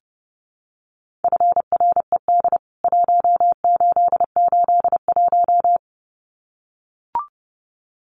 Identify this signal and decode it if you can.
Morse code, heard as a keyed tone, about 30 wpm, 700 Hz. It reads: FRED 1881
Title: CWT Exchange 30wpm